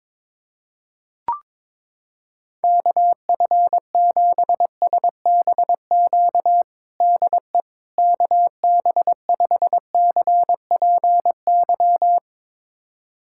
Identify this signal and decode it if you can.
Morse code, heard as a keyed tone, about 22 wpm, 700 Hz. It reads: KF7SBQ DE KB5CPY
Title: USA Callsign DE Callsign 22wpm